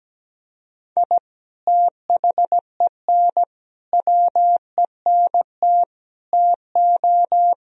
Morse code, heard as a keyed tone, about 17 wpm, 700 Hz. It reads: I THEN WENT TO
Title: Sentences from Top 1000 Words - Review Part 3 - 17wpm